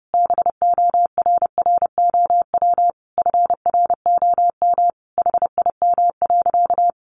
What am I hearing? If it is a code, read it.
Morse code, heard as a keyed tone, about 28 wpm, 700 Hz. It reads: BORROW FROM HIM.
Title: Sentences from Top 1500 Words 30wpm